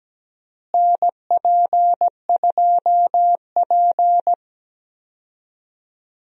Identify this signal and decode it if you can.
Morse code, heard as a keyed tone, about 17 wpm, 700 Hz. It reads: NP2P
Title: USA Call Signs - 4 Characters - Repeated 3x - 17wpm